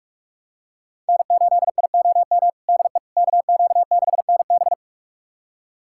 Morse code, heard as a keyed tone, about 41 wpm, 700 Hz. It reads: N8IOM DE KQ6NB
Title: USA Callsign DE Callsign - 5 Characters - 45wpm